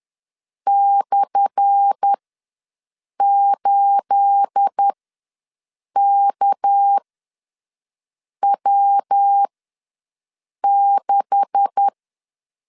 Morse code, heard as a keyed tone, about 10 wpm, spaced as for 5 wpm, 800 Hz. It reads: /8KW6
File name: LEZIONI / 10 SIGNS / 10A SLASH ws 10wpm 4121